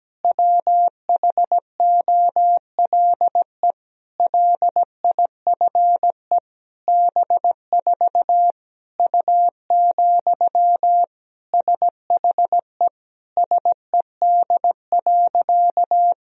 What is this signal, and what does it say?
Morse code, heard as a keyed tone, about 17 wpm, 700 Hz. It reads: WHOLE LIFE B4 U, SHE SED.
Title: Sentences from Top 400 Words and Abbreviations 17wpm